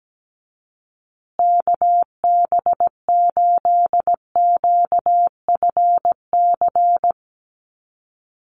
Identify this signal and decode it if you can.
Morse code, heard as a keyed tone, about 17 wpm, 700 Hz. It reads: KB8QFC
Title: USA Call Signs Repeated 2x 17wpm